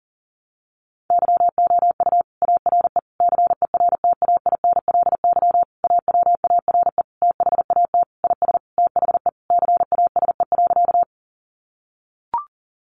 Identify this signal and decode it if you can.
Morse code, heard as a keyed tone, about 40 wpm, 700 Hz. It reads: YOU ARE CERTAINLY AWARE THAT IS THE CASE.
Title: Sentences from Top 1200 Words 40wpm